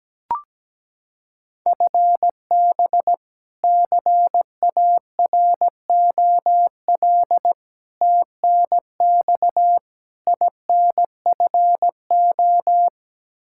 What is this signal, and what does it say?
Morse code, heard as a keyed tone, about 17 wpm, 700 Hz. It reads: FB CAROL TNX INFO
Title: K1USN SST QSO Practice 17wpm